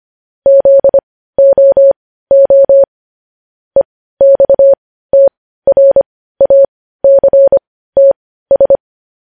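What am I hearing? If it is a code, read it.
Morse code, heard as a keyed tone, about 25 wpm, spaced as for 15 wpm, 550 Hz. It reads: ZOO EXTRACTS
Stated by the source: Morse news headlines sent at 15 words per minute.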